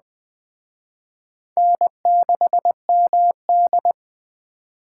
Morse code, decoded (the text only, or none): N6MD